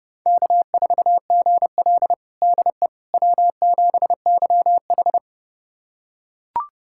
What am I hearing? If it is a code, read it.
Morse code, heard as a keyed tone, about 31 wpm, 700 Hz. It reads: K4GL DE W7YH
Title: USA Callsign DE Callsign - 4 Characters - 30wpm